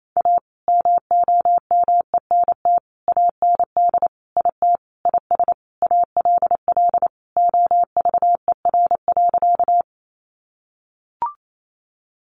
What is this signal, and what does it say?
Morse code, heard as a keyed tone, about 28 wpm, 700 Hz. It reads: A MOMENT AND IT IS ALL OVER.
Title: Sentences from Top 1000 Words 28wpm